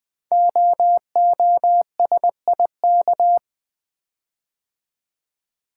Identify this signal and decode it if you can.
Morse code, heard as a keyed tone, about 20 wpm, 700 Hz. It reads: OOSIK
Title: Code Groups - 5 Letters - 20wpm